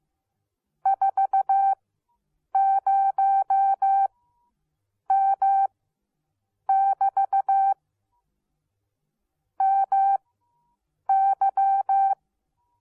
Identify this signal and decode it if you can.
Morse code, heard as a keyed tone, about 15 wpm, spaced as for 7 wpm, 780 Hz. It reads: 40M <BT> MY